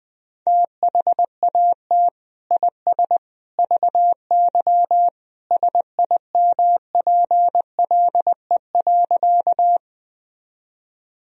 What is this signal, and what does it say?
Morse code, heard as a keyed tone, about 20 wpm, 700 Hz. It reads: THAT IS VY SIMPLE.
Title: Sentences from Top 400 Words and Abbreviations 20wpm